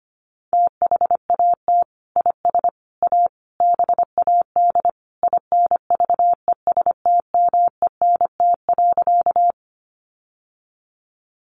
Morse code, heard as a keyed tone, about 25 wpm, 700 Hz. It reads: THAT IS A BAD INVESTMENT.
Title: Sentences from Top 700 Words 25wpm